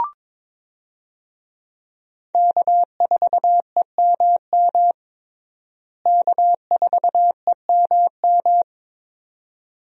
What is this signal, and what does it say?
Morse code, heard as a keyed tone, about 22 wpm, 700 Hz. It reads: K4EMM K4EMM
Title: USA Call Signs - 5 Characters - Repeated 2x - 22wpm